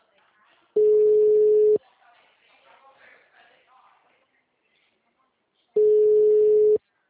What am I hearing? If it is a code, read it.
Ringback tone (Europe, 425 Hz), with speech.
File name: out- -2012- - - .5266